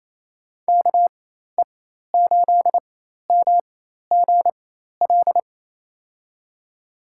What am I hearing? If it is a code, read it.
Morse code, heard as a keyed tone, about 27 wpm, spaced as for 13 wpm, 700 Hz. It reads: KE8MGL
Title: USA Call Signs - 6 Characters - 4x Character Spacing - 28wpm